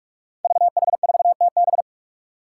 Morse code, heard as a keyed tone, about 44 wpm, 700 Hz. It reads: VH4TB
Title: Five Challenging Letter-Number Combinations 45wpm